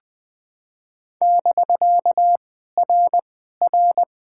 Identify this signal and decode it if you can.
Morse code, heard as a keyed tone, about 18 wpm, 700 Hz. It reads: <BK> R R
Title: POTA QSO Practice - 20wpm